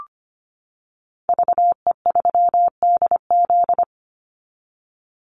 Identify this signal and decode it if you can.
Morse code, heard as a keyed tone, about 25 wpm, 700 Hz. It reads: VE3DZ
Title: Call Signs - 25wpm